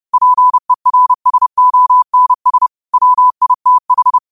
Morse code, heard as a keyed tone, about 30 wpm, 1000 Hz. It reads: PERSONS WITH
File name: qotdfiles / 30wpm